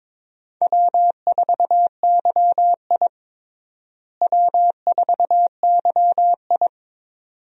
Morse code, heard as a keyed tone, about 22 wpm, 700 Hz. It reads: W4YI W4YI
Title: USA Call Signs - 4 Characters - Repeated 2x - 22wpm